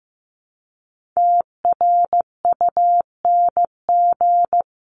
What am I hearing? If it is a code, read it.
Morse code, heard as a keyed tone, about 15 wpm, 700 Hz. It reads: TRUNG
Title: CWT Exchange 15wpm